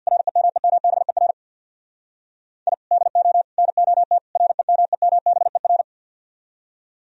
Morse code, heard as a keyed tone, about 64 wpm, 700 Hz. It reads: REMEMBER I DO NOT REMEMBER